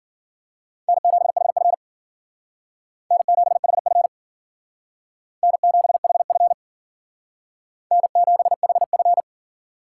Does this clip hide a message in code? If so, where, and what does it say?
Yes: Morse code from 0.9 to 9.2 s, heard as a keyed tone, about 51 wpm, 700 Hz. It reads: N7HF N7HF N7HF N7HF